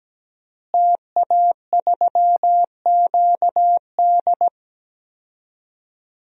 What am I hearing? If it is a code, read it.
Morse code, heard as a keyed tone, about 17 wpm, 700 Hz. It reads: TA3QD
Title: DXCC-100 Call Signs - 5 Characters - Repeated 3x - 17wpm